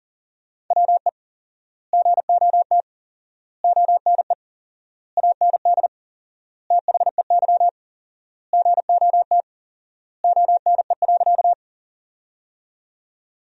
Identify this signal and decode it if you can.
Morse code, heard as a keyed tone, about 40 wpm, 700 Hz. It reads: WE GOT ONE AND THEY GOT ONE.
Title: 4x Word-Spacing - Sentences from Top 300 Words - 40wpm